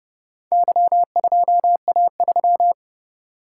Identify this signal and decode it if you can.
Morse code, heard as a keyed tone, about 28 wpm, 700 Hz. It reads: Y2A3